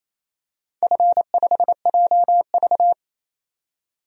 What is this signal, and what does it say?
Morse code, heard as a keyed tone, about 28 wpm, 700 Hz. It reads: F5JV
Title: DXCC-100 Call Signs - 4 Characters - 28wpm